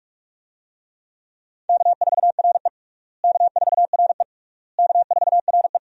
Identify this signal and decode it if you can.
Morse code, heard as a keyed tone, about 41 wpm, 700 Hz. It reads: K4RE K4RE K4RE
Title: USA Call Signs - 4 Characters - Repeated 3x - 45wpm